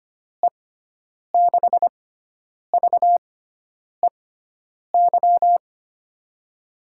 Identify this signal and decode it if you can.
Morse code, heard as a keyed tone, about 25 wpm, spaced as for 9 wpm, 700 Hz. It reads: E6VEY
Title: USA Call Signs - 6 Characters - 6x Character Spacing - 25wpm